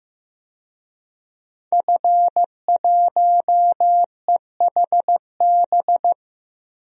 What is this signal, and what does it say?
Morse code, heard as a keyed tone, about 15 wpm, 700 Hz. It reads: F1EHB
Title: DXCC-100 Call Signs - 5 Characters - 15wpm